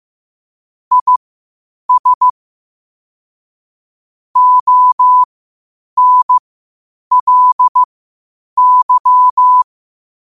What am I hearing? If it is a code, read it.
Morse code, heard as a keyed tone, about 15 wpm, spaced as for 8 wpm, 1000 Hz. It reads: IS ONLY
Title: 8 WPM morse code quotes for Fri, 01 Mar 2024